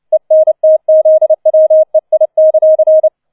Morse code, heard as a keyed tone, about 29 wpm, 630 Hz. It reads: ENTZWEI;
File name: 006 Sechster Streich 30wpm